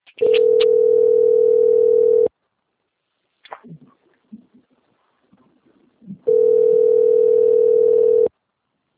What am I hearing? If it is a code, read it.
Ringback tone (North America), 440 + 480 Hz.